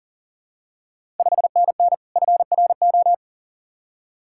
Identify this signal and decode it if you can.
Morse code, heard as a keyed tone, about 38 wpm, 690 Hz. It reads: 5NN FRO
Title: Ontario QSO Party Exchange 40wpm